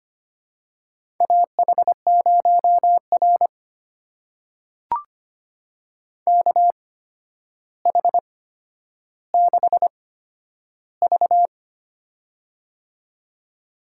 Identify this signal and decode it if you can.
Morse code, heard as a keyed tone, about 25 wpm, spaced as for 11 wpm, 700 Hz. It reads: AH0R KH6V
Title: USA Call Signs - 8x Character Spacing - 25wpm